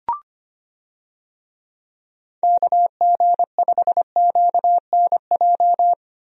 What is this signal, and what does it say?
Morse code, heard as a keyed tone, about 25 wpm, 700 Hz. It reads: KG5QNJ
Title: USA Call Signs - 6 Characters - 25wpm